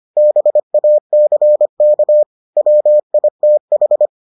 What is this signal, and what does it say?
Morse code, heard as a keyed tone, about 25 wpm, 600 Hz. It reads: BACK WITH